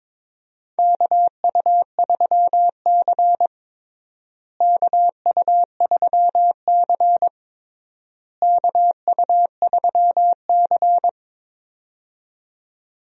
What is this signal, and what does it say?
Morse code, heard as a keyed tone, about 21 wpm, 700 Hz. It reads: KU3C KU3C KU3C
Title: USA Call Signs - 4 Characters - Repeated 3x - 22wpm